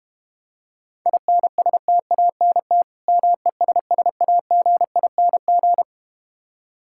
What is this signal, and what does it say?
Morse code, heard as a keyed tone, about 32 wpm, 700 Hz. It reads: INSTANT MESSAGING